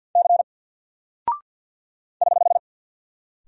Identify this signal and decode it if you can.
Morse code, heard as a keyed tone, about 48 wpm, 700 Hz. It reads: C <HH>
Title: Single Character (Letter:Number:Punctuation:Prosign) Alt 50wpm